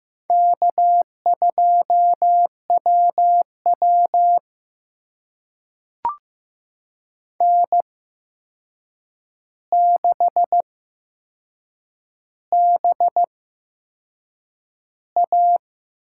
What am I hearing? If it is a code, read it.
Morse code, heard as a keyed tone, about 15 wpm, spaced as for 6 wpm, 700 Hz. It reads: K2WW N6BA
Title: USA Call Signs - 4 Characters - 8x Character Spacing - 15wpm